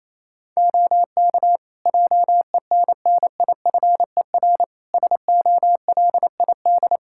Morse code, heard as a keyed tone, about 27 wpm, 700 Hz. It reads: OK JENNIFER SOLID
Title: Practice QSO - Exchange 3 - More Info - 28wpm